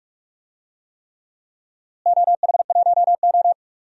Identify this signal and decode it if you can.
Morse code, heard as a keyed tone, about 47 wpm, 700 Hz. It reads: OH1O
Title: DXCC-100 Call Signs - 4 Characters - Repeated 3x - 45wpm